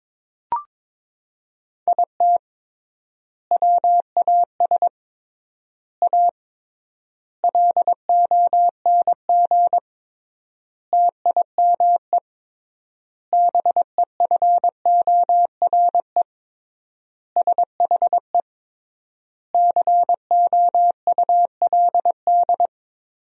Morse code, heard as a keyed tone, about 22 wpm, 700 Hz. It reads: IT WAS A LONG TIME BEFORE SHE COULD
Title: 3x Word-Spacing - Sentences from Top 500 Words - 22wpm